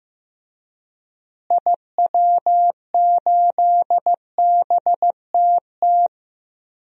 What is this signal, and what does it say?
Morse code, heard as a keyed tone, about 15 wpm, 700 Hz. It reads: IW8BTT
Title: DXCC-100 Call Signs - 6 Characters - Repeated 3x - 15wpm